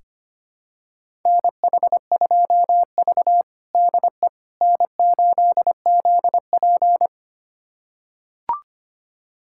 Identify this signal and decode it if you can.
Morse code, heard as a keyed tone, about 25 wpm, 700 Hz. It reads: NH2V DE N8ZP
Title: USA Callsign DE Callsign - 4 Characters - 25wpm